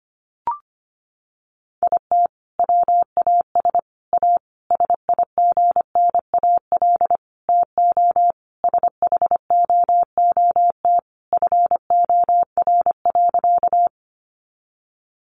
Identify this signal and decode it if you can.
Morse code, heard as a keyed tone, about 25 wpm, 700 Hz. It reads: IT WAS A SIGNAL TO SHOOT FOR.
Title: Sentences from Top 1400 Words 25wpm